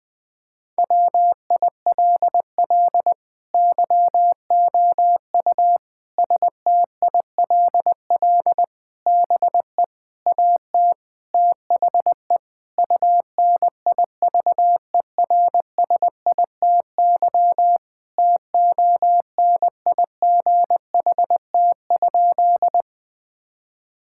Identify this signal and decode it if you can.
Morse code, heard as a keyed tone, about 20 wpm, 700 Hz. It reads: WILL YOU STILL BE AT THE UNIVERSITY TONIGHT?
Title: Sentences from Top 1000 Words - Review Part 5 - 20wpm